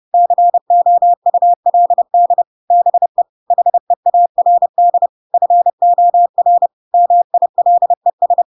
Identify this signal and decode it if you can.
Morse code, heard as a keyed tone, about 30 wpm, 700 Hz. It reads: COULD BE HEARD FOR MILES
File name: Pinocchio / Pinocchio ch7 30 30 ik1hge